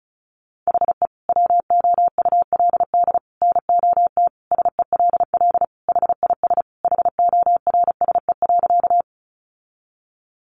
Morse code, heard as a keyed tone, about 34 wpm, 700 Hz. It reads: HE WOULD NOT SELL HIS HORSE.
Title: Sentences from Top 1100 Words 35wpm